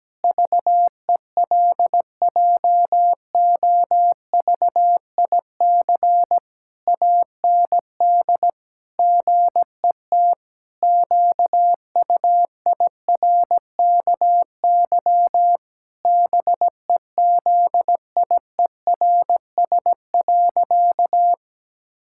Morse code, heard as a keyed tone, about 17 wpm, 700 Hz. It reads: VELJOVIC AND GET QUIRKY BEZIERS.
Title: Pangram Challenge v2 - 17wpm